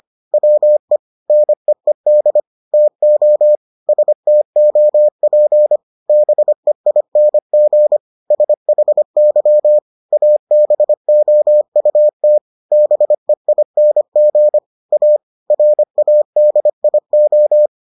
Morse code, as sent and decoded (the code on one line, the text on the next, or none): .-- . / -. . -.. / - --- / ... - --- .--. / -... .. -. --. / ... .... -.-- / .- -... --- ..- - / -... .. -. --. / .- / .-. .- -.. .. ---
WE NEED TO STOP BEING SHY ABOUT BEING A RADIO